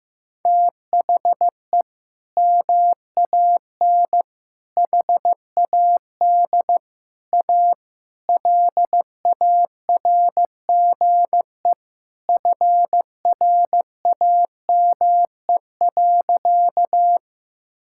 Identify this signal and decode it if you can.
Morse code, heard as a keyed tone, about 15 wpm, 700 Hz. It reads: THE MAN HAD A LARGE FRAME.
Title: Sentences from Top 1000 Words - Review Part 6 - 15wpm